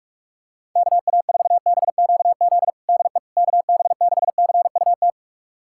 Morse code, heard as a keyed tone, about 42 wpm, 700 Hz. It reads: KA4BQZ DE KB6CUT
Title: USA Callsign DE Callsign - 6 Characters - 45wpm